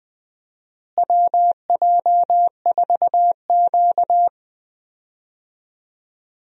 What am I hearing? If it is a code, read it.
Morse code, heard as a keyed tone, about 20 wpm, 700 Hz. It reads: WJ4Q
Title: USA Call Signs - 4 Characters - Repeated 3x - 20wpm